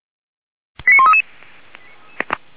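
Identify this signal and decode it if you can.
Five-tone selective call (ZVEI), heard as quick runs of five tones, tones of about 70 ms. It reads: 89133
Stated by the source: ZVEI five-tone selective call received by SDR for feuerwehr-orlamuende.org (fire brigade Orlamünde).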